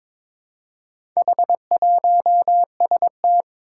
Morse code, heard as a keyed tone, about 22 wpm, 700 Hz. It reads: H1ST
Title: Four Challenging Letter-Number Combinations 22wpm